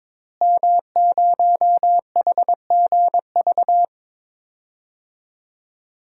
Morse code, heard as a keyed tone, about 22 wpm, 700 Hz. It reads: M0HGV
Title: DXCC-100 Call Signs - 5 Characters - Repeated 3x - 22wpm